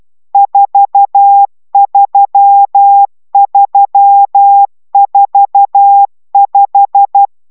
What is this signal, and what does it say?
Morse code, heard as a keyed tone, about 12 wpm, 800 Hz. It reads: 43345